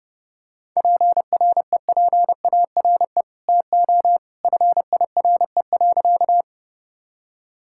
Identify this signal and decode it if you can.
Morse code, heard as a keyed tone, about 30 wpm, 700 Hz. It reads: PREPARE TO FIRE.
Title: Sentences from Top 1000 Words - Review Part 6 - 30wpm